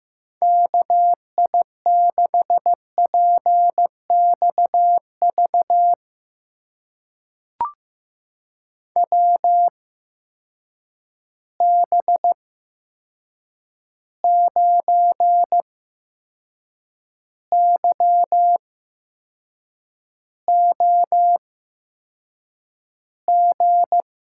Morse code, heard as a keyed tone, about 15 wpm, spaced as for 6 wpm, 700 Hz. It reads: KI6PXV WB9YOG
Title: USA Call Signs - 6 Characters - 8x Character Spacing - 15wpm